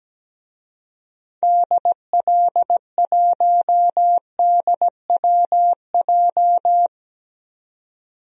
Morse code, heard as a keyed tone, about 17 wpm, 700 Hz. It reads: DL1DWJ
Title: DXCC-100 Call Signs - 6 Characters - 17wpm